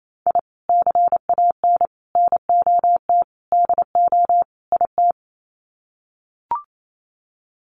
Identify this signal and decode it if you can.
Morse code, heard as a keyed tone, about 28 wpm, 700 Hz. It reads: I CAN NOT DO IT
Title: Sets of 5 Words Encore 28wpm